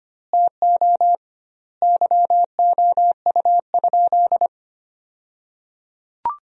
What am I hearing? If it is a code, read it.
Morse code, heard as a keyed tone, about 25 wpm, 700 Hz. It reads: TO YOU?
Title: 2x Word-Spacing - Sentences from Top 100 Words - 25wpm